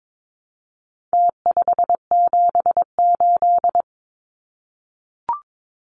Morse code, heard as a keyed tone, about 22 wpm, 700 Hz. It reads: T578
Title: Numbers - Contest Style - 4-Digit 22wpm